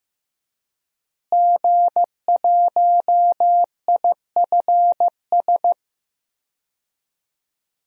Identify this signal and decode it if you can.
Morse code, heard as a keyed tone, about 15 wpm, 700 Hz. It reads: G1IFS
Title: DXCC-100 Call Signs - 5 Characters - Repeated 3x - 15wpm